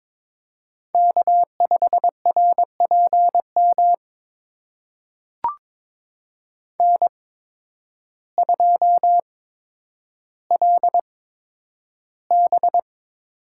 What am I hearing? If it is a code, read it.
Morse code, heard as a keyed tone, about 22 wpm, spaced as for 10 wpm, 700 Hz. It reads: K5RPM N2LB
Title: USA Call Signs - 5 Characters - 8x Character Spacing - 22wpm